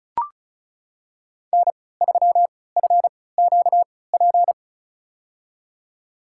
Morse code, heard as a keyed tone, about 36 wpm, spaced as for 20 wpm, 700 Hz. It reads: N3FQP
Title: USA Call Signs - 5 Characters - 3x Character Spacing - 35wpm